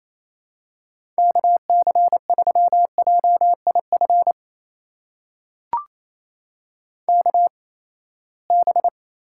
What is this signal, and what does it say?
Morse code, heard as a keyed tone, about 28 wpm, spaced as for 19 wpm, 700 Hz. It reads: KC3JIF KB
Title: USA Call Signs - 6 Characters - 8x Character Spacing - 28wpm